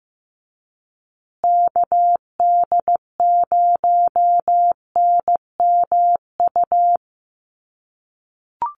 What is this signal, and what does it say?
Morse code, heard as a keyed tone, about 15 wpm, 700 Hz. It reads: KD0NMU
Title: USA Call Signs Repeated 2x 15wpm